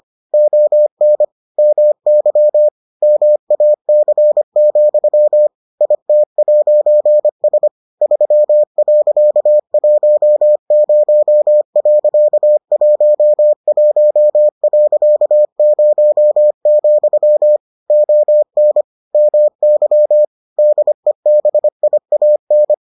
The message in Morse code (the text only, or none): ON MY MAC, IT'S 3.10.11.0, ON MY DEBIAN